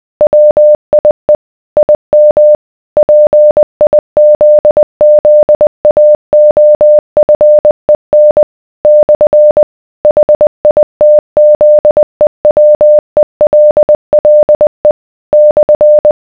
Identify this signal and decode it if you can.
Morse code, heard as a keyed tone, about 20 wpm, 600 Hz. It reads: WIE IM PIZZAOFEN / HITZEWELLE /